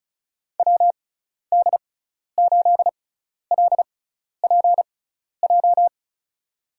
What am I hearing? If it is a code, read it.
Morse code, heard as a keyed tone, about 36 wpm, spaced as for 12 wpm, 700 Hz. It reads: WD8LPJ
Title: USA Call Signs - 6 Characters - 6x Character Spacing - 35wpm